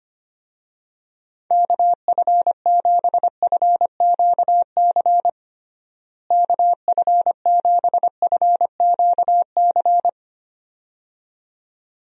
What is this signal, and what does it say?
Morse code, heard as a keyed tone, about 25 wpm, 700 Hz. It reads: KF7FQC KF7FQC
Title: USA Call Signs - 6 Characters - Repeated 2x - 25wpm